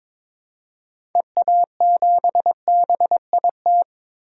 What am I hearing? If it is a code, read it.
Morse code, heard as a keyed tone, about 22 wpm, 700 Hz. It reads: EA7BIT
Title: DXCC-100 Call Signs - 6 Characters - Repeated 2x - 22wpm